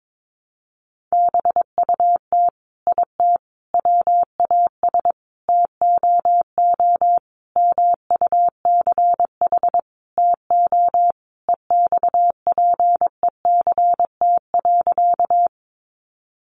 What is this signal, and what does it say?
Morse code, heard as a keyed tone, about 22 wpm, 700 Hz. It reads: BUT IT WAS TOO MUCH TO EXPECT.
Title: Sentences from Top 1100 Words 22wpm